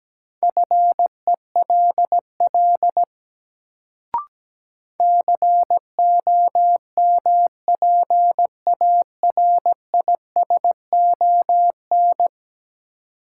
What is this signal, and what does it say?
Morse code, heard as a keyed tone, about 17 wpm, 700 Hz. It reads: FELL COMPARISON
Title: Sentences from Top 1600 Words - New Word Drill - 17wpm